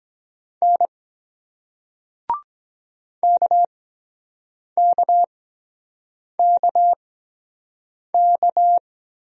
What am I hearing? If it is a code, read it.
Morse code, heard as a keyed tone, about 23 wpm, spaced as for 7 wpm, 700 Hz. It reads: N KKKK